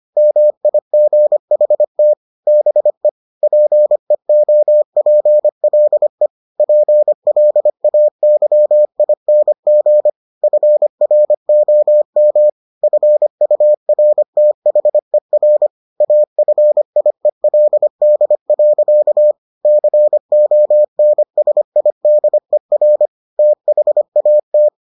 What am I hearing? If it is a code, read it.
Morse code, heard as a keyed tone, about 25 wpm, 600 Hz. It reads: MIGHT BE PEOPLE PLAYING FROM FURTHER AFIELD. CONSIDER THAT